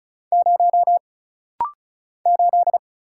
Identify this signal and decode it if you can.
Morse code, heard as a keyed tone, about 35 wpm, 700 Hz. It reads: 0 8
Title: Single Letter-Number Alt - Rapid-Fire 35wpm